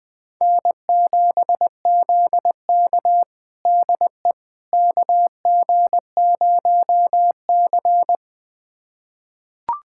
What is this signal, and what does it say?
Morse code, heard as a keyed tone, about 21 wpm, 700 Hz. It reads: N7ZK DE KG0C
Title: USA Callsign DE Callsign - 4 Characters - 20wpm